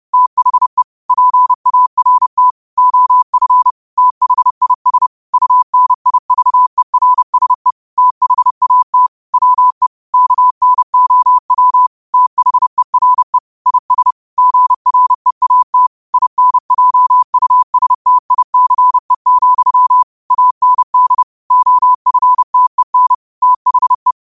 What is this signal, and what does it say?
Morse code, heard as a keyed tone, about 30 wpm, 1000 Hz. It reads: THE PART OF THIS UNIVERSE THAT WE KNOW THERE IS GREAT INJUSTICE, AND OFTEN THE
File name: qotdfiles / 30wpm